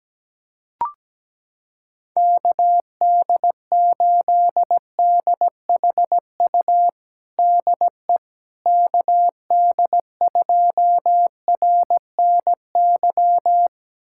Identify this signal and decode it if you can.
Morse code, heard as a keyed tone, about 17 wpm, 700 Hz. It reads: KD8DHU DE KD2RNY
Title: USA Callsign DE Callsign 17wpm